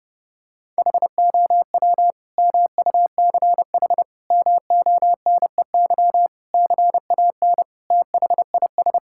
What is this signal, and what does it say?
Morse code, heard as a keyed tone, about 30 wpm, 700 Hz. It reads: HOW MUCH MONEY CAN THIS
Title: Sentences from Top 1000 Words - Review Part 5 - 30wpm